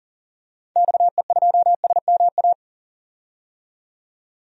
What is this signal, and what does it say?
Morse code, heard as a keyed tone, about 37 wpm, 700 Hz. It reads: XE2SMA
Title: DXCC-100 Call Signs - 6 Characters - 40wpm